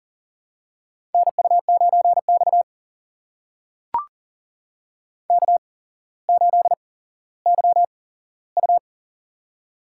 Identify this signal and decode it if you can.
Morse code, heard as a keyed tone, about 41 wpm, spaced as for 17 wpm, 700 Hz. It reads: NU9X K8YU
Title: USA Call Signs - 4 Characters - 8x Character Spacing - 40wpm